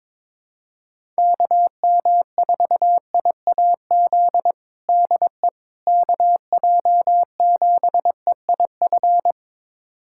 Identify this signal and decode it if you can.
Morse code, heard as a keyed tone, about 22 wpm, 700 Hz. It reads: KM4IAZ DE KJ7EIF
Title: USA Callsign DE Callsign 22wpm